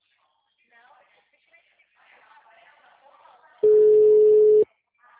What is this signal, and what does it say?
Ringback tone (Europe, 425 Hz), with speech.